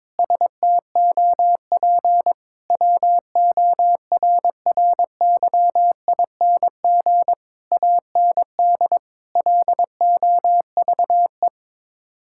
Morse code, heard as a keyed tone, about 22 wpm, 700 Hz. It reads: STOP WORRYING AND LOVE
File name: practice / 22wpm / Sets of 8 Words - 22wpm